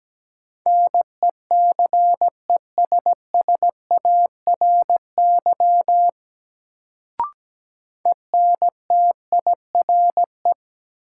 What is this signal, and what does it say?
Morse code, heard as a keyed tone, about 17 wpm, 700 Hz. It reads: NECESSARY ENTIRE
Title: Sentences from Top 800 Words - New Word Drill - 17wpm